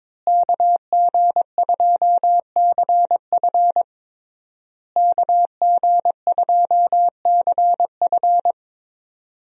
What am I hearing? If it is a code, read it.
Morse code, heard as a keyed tone, about 21 wpm, 700 Hz. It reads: KG2CF KG2CF
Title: USA Call Signs - 5 Characters - Repeated 3x - 22wpm